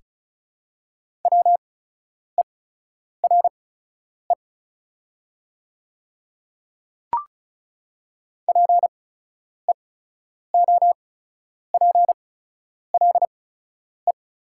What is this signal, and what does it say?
Morse code, heard as a keyed tone, about 36 wpm, spaced as for 10 wpm, 690 Hz. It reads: WERE PEOPLE
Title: 8x Character Spacing - Top 100 Words - 35wpm